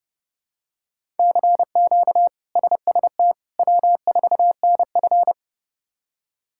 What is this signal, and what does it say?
Morse code, heard as a keyed tone, about 30 wpm, 700 Hz. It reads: CQ SST W4NF
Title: K1USN SST QSO Practice - 30wpm